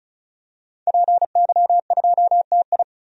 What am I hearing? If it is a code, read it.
Morse code, heard as a keyed tone, about 34 wpm, 700 Hz. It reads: PY2TI
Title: DXCC-100 Call Signs - 5 Characters - 35wpm